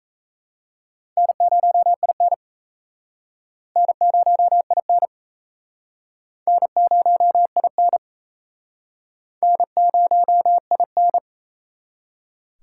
Morse code, heard as a keyed tone, about 36 wpm, 700 Hz. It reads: N0IN N0IN N0IN N0IN